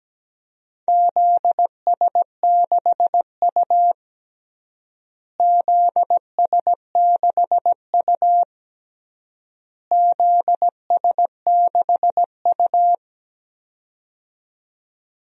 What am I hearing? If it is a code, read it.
Morse code, heard as a keyed tone, about 17 wpm, 700 Hz. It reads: ZS6U ZS6U ZS6U
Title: DXCC-100 Call Signs - 4 Characters - Repeated 3x - 17wpm